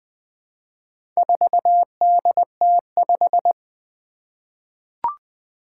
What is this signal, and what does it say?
Morse code, heard as a keyed tone, about 20 wpm, 700 Hz. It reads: 4DT5
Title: Four Challenging Letter-Number Combinations 20wpm